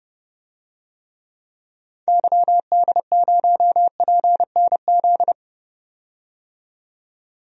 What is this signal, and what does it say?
Morse code, heard as a keyed tone, about 31 wpm, 700 Hz. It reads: YD0PNZ
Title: DXCC-100 Call Signs - 6 Characters - 30wpm